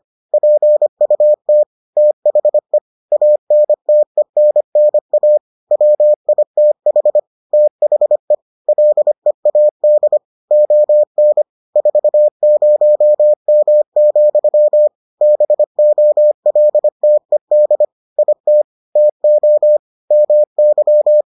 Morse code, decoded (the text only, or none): PUT THE ANTENNA WITH THE LEAD ON 40M, BOLTED IT TO MY